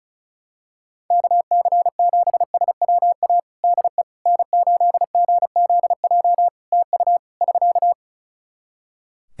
Morse code, heard as a keyed tone, about 36 wpm, 700 Hz. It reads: KC7SWA DE N8GZJ TU <SK>
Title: Practice QSO - Exchange 4 - Ending - 35wpm